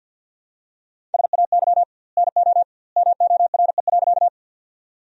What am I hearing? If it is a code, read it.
Morse code, heard as a keyed tone, about 49 wpm, 700 Hz. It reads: SAY NO MORE.
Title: 2x Word-Spacing - Sentences from Top 200 Words - 50wpm